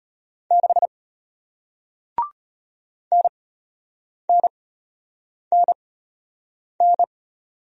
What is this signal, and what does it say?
Morse code, heard as a keyed tone, about 33 wpm, spaced as for 8 wpm, 700 Hz. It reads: 6 NNNN